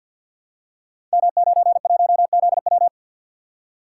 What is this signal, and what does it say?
Morse code, heard as a keyed tone, about 51 wpm, 700 Hz. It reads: M91ZW
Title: Code Groups - 5 Letter:Numbers - 50wpm